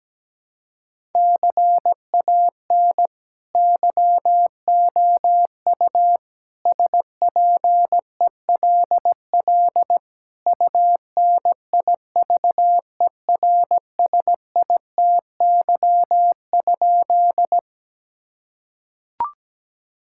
Morse code, heard as a keyed tone, about 17 wpm, 700 Hz. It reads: CAN YOU SPELL UNIVERSITY?
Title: Sentences from Top 1000 Words - Review Part 5 - 17wpm